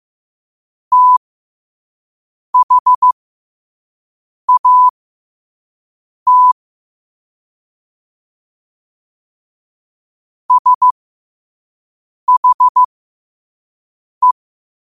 Morse code, heard as a keyed tone, about 15 wpm, spaced as for 5 wpm, 1000 Hz. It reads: THAT SHE
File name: qotdfiles / 5wpm